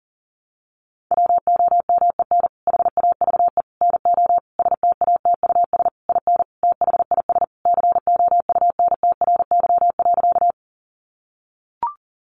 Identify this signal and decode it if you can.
Morse code, heard as a keyed tone, about 40 wpm, 700 Hz. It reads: WOMEN HAVE NO STATUS IN THIS COUNTRY.
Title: Sentences from Top 1400 Words 40wpm